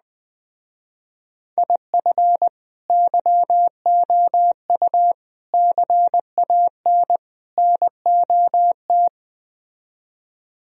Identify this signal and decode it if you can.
Morse code, heard as a keyed tone, about 20 wpm, 700 Hz. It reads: IF YOU CAN NOT
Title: Speed-Racing - Sets of 4 Words 17wpm